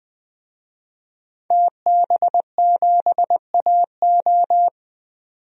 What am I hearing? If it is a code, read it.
Morse code, heard as a keyed tone, about 20 wpm, 700 Hz. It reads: TB7AO
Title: DXCC-100 Call Signs - 5 Characters - 20wpm